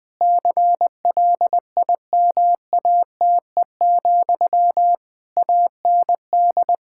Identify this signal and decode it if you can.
Morse code, heard as a keyed tone, about 20 wpm, 700 Hz. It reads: CLIMATE, AND
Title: Sentences from Top 1600 Words 20wpm